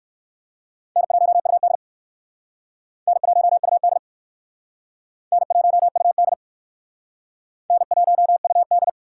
Morse code, heard as a keyed tone, about 56 wpm, 700 Hz. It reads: N1UD N1UD N1UD N1UD
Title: Speed Racing - USA 1x2 Call Signs 45wpm